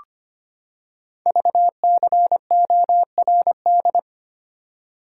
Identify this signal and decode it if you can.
Morse code, heard as a keyed tone, about 25 wpm, 700 Hz. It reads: VCORD
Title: Code Groups - 5 Letters - 25wpm